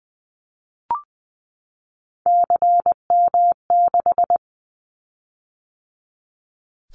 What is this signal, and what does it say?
Morse code, heard as a keyed tone, about 20 wpm, 700 Hz. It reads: CM6
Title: Callsign Prefixes for Top 100 DXCC - 20wpm